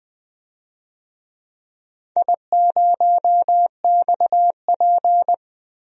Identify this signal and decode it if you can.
Morse code, heard as a keyed tone, about 21 wpm, 700 Hz. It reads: I0XP
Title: DXCC-100 Call Signs - 4 Characters - Repeated 2x - 20wpm